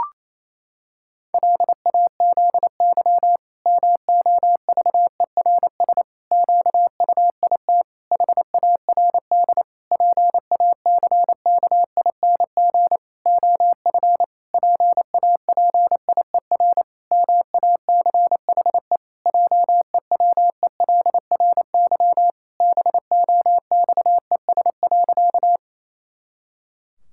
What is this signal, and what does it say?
Morse code, heard as a keyed tone, about 28 wpm, 700 Hz. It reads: LAZY MOVERS QUIT HARD PACKING OF PAPIER MACHE JEWELRY BOXES.
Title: Pangram Challenge v2 - 28wpm